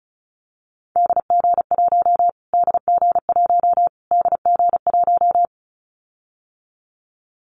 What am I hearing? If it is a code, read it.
Morse code, heard as a keyed tone, about 34 wpm, 700 Hz. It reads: DG1 DG1 DG1
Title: Callsign Prefixes for Top 25 DXCC - Repeated 3x - 35wpm